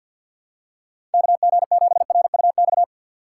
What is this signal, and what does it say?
Morse code, heard as a keyed tone, about 47 wpm, 700 Hz. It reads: KG7RUX
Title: USA Call Signs - 6 Characters - 6x Character Spacing - 50wpm